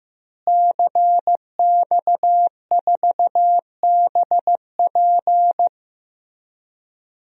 Morse code, heard as a keyed tone, about 15 wpm, 700 Hz. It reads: CX4BP
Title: DXCC-100 Call Signs - 5 Characters - Repeated 3x - 15wpm